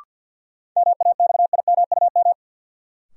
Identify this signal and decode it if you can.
Morse code, heard as a keyed tone, about 51 wpm, 700 Hz. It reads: MAXIMUM
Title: Sentences from Top 1300 Words - New Word Drill Alt - 50wpm